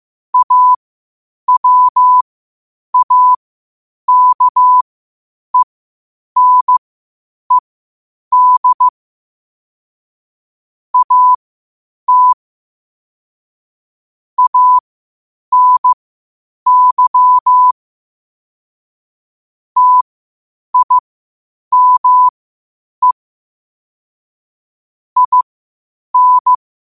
Morse code, heard as a keyed tone, about 15 wpm, spaced as for 8 wpm, 1000 Hz. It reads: AWAKENED AT ANY TIME IN
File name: qotdfiles / 8wpm